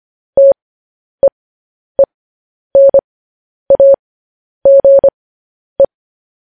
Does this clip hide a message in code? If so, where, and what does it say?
Yes: Morse code from 0.4 to 5.8 s, heard as a keyed tone, about 25 wpm, spaced as for 10 wpm, 550 Hz. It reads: TEENAGE